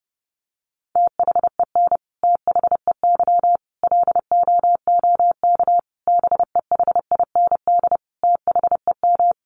Morse code, heard as a keyed tone, about 30 wpm, 700 Hz. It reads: THEN THEY LOOK BEHIND THEM
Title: Sentences from Top 1300 Words 30wpm